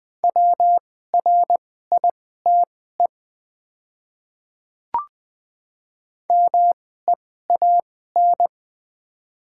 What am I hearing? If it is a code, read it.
Morse code, heard as a keyed tone, about 20 wpm, spaced as for 14 wpm, 700 Hz. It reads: WRITE MEAN
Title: 2x Character Spacing - Top 200 Words - 20wpm